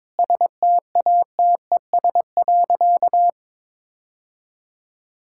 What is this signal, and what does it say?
Morse code, heard as a keyed tone, about 22 wpm, 700 Hz. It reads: STATES.
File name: practice / 22wpm / Sentences with 2-Word Phrases - 22wpm